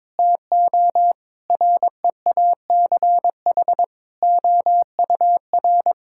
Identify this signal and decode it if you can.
Morse code, heard as a keyed tone, about 22 wpm, 700 Hz. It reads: TO REACH OUR
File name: Sentences with 7-Word Phrases - 22wpm